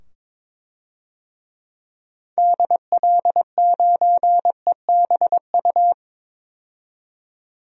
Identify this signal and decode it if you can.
Morse code, heard as a keyed tone, about 22 wpm, 700 Hz. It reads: DL9EBU
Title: DXCC-100 Call Signs - 6 Characters - 22wpm